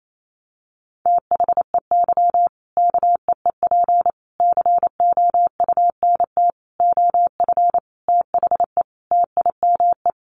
Morse code, heard as a keyed tone, about 28 wpm, 700 Hz. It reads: THEY KEEP COUNT OF THE TIME
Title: Sentences from Top 1500 Words 28wpm